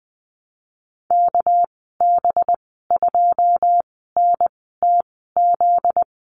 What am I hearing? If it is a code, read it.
Morse code, heard as a keyed tone, about 20 wpm, spaced as for 15 wpm, 700 Hz. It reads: KB2NTZ
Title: 245 Lesson 39g - Letter z - Call Signs 20wpm